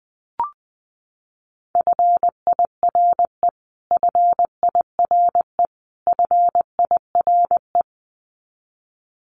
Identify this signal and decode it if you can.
Morse code, heard as a keyed tone, about 20 wpm, 700 Hz. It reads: FIRE FIRE FIRE
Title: Top 500 Words Repeated 20wpm